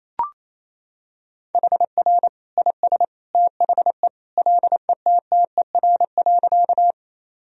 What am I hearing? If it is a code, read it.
Morse code, heard as a keyed tone, about 28 wpm, 700 Hz. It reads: HR IS THE LETTER.
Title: Sentences from Top 300 Words and Abbreviations 28wpm